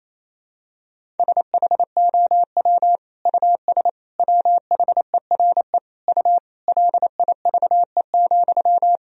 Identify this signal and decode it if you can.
Morse code, heard as a keyed tone, about 28 wpm, 700 Hz. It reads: SHOW US WHERE U LIVE,
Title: Sentences from Top 200 Words and Abbreviations 28wpm